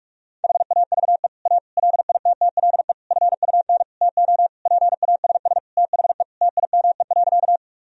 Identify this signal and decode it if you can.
Morse code, heard as a keyed tone, about 46 wpm, 700 Hz. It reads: HAVE A LITTLE FUN TO PASS THE TIME.
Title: Sentences from Top 1000 Words - Review Part 4 - 45wpm